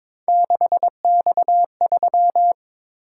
Morse code, heard as a keyed tone, about 22 wpm, 700 Hz. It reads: 6X3
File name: practice / 22wpm / Three Letter-Number Combinations - 22wpm